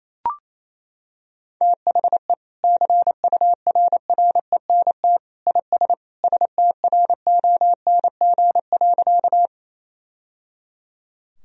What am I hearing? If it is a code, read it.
Morse code, heard as a keyed tone, about 28 wpm, 700 Hz. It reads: THE CURRENT IS STRONG.
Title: Sentences from Top 1000 Words - Review Part 4 - 28wpm